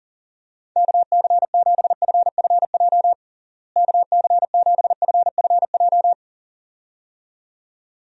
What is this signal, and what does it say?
Morse code, heard as a keyed tone, about 37 wpm, 700 Hz. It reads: KC7FFJ KC7FFJ
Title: USA Call Signs - 6 Characters - Repeated 2x - 40wpm